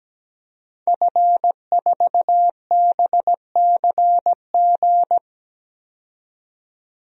Morse code, heard as a keyed tone, about 17 wpm, 700 Hz. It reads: F4BCG
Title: DXCC-100 Call Signs - 5 Characters - 17wpm